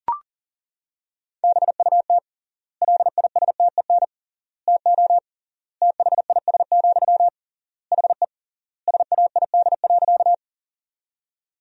Morse code, heard as a keyed tone, about 40 wpm, 700 Hz. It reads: BUT LISTEN TO THIS, HE SAID.
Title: 3x Word-Spacing - Sentences from Top 400 Words - 40wpm